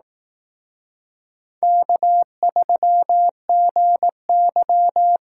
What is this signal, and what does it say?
Morse code, heard as a keyed tone, about 18 wpm, 700 Hz. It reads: K3GY
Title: Speed Racing - USA Call Signs 15wpm